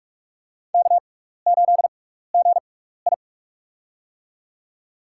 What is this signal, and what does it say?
Morse code, heard as a keyed tone, about 46 wpm, spaced as for 16 wpm, 700 Hz. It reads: K8GI
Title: USA Call Signs - 6x Character Spacing - 45wpm